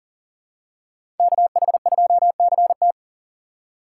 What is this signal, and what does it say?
Morse code, heard as a keyed tone, about 40 wpm, 700 Hz. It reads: KH2CT
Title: USA Call Signs - 4x Character Spacing - 40wpm